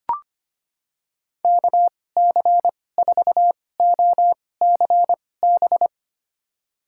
Morse code, heard as a keyed tone, about 25 wpm, spaced as for 18 wpm, 700 Hz. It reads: KC4OCB
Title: USA Call Signs - 6 Characters - 2x Character Spacing - 25wpm